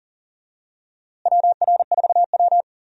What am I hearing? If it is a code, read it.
Morse code, heard as a keyed tone, about 40 wpm, 700 Hz. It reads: WR4W
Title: USA Call Signs - 4 Characters - Repeated 3x - 40wpm